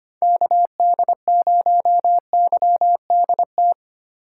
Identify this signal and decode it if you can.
Morse code, heard as a keyed tone, about 25 wpm, 700 Hz. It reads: KD0YDT
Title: USA Call Signs - 6 Characters - 3x Character Spacing - 25wpm